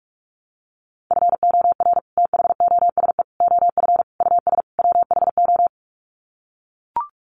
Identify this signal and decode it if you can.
Morse code, heard as a keyed tone, about 44 wpm, 700 Hz. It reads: FOR THOSE OF US WHO
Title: Sets of 5 Words Encore 45wpm